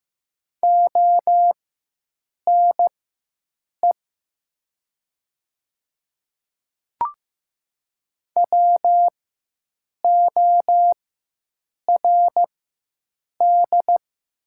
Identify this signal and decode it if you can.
Morse code, heard as a keyed tone, about 15 wpm, spaced as for 7 wpm, 700 Hz. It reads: ONE WORD
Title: 4x Character Spacing - Top 300 Words - 15wpm